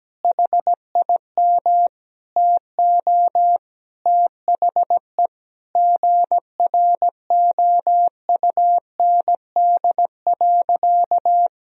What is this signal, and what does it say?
Morse code, heard as a keyed tone, about 17 wpm, 700 Hz. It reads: HIM TO THE GROUND.
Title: Sentences from Top 1000 Words - Review Part 3 - 17wpm